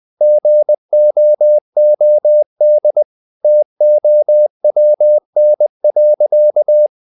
Morse code, heard as a keyed tone, about 20 wpm, 600 Hz. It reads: GOOD TOWN.
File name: the old man and the sea part 2 20wpm